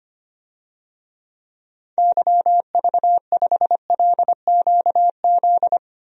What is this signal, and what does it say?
Morse code, heard as a keyed tone, about 25 wpm, 700 Hz. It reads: YV5LQZ
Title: DXCC-100 Call Signs - 6 Characters - Repeated 2x - 25wpm